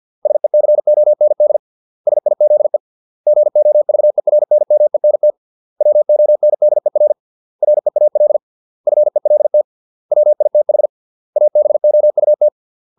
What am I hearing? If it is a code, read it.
Morse code, heard as a keyed tone, about 50 wpm, 600 Hz. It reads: SECOND SIZE GOVERNMENT WONDER REAL FELT WITH ABOUT